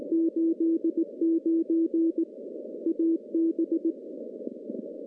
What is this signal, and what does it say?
Morse code, heard as a keyed tone, about 18 wpm, 330 Hz. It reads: 89 AB
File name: contests / CW-SS-k5tr- -0700